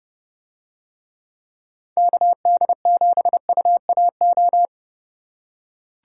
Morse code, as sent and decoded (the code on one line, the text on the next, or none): -.- -.. --... ..- .- ---
KD7UAO